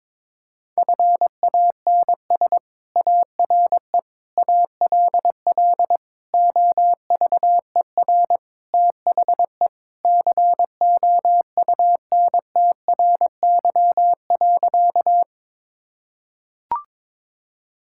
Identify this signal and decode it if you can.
Morse code, heard as a keyed tone, about 22 wpm, 700 Hz. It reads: FANS ARE ALL OVER THE COUNTRY.
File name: practice / 22wpm / Sentences with 4-Word Phrases - 22wpm